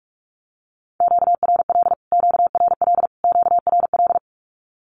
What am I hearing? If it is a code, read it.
Morse code, heard as a keyed tone, about 44 wpm, 700 Hz. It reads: QRL QRL QRL
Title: Q-Codes Repeated 3x 45wpm